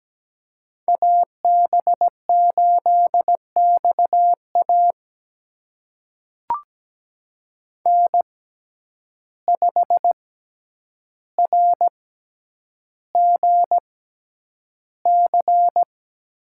Morse code, heard as a keyed tone, about 17 wpm, spaced as for 9 wpm, 700 Hz. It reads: AB8XA N5RGC
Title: USA Call Signs - 5 Characters - 6x Character Spacing - 17wpm